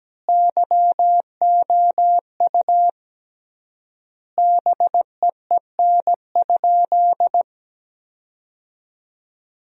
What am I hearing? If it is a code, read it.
Morse code, heard as a keyed tone, about 17 wpm, 700 Hz. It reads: YOU BEEN?
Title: 3x Word-Spacing - Sentences from Top 400 Words - 17wpm